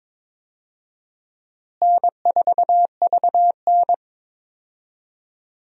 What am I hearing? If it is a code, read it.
Morse code, heard as a keyed tone, about 22 wpm, 700 Hz. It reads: N4VN
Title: USA Call Signs - 4 Characters - Repeated 2x - 22wpm